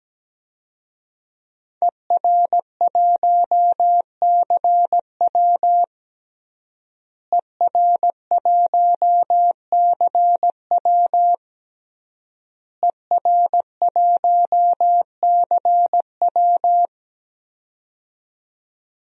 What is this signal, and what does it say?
Morse code, heard as a keyed tone, about 17 wpm, 700 Hz. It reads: ER1CW ER1CW ER1CW
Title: DXCC-100 Call Signs - 5 Characters - Repeated 3x - 17wpm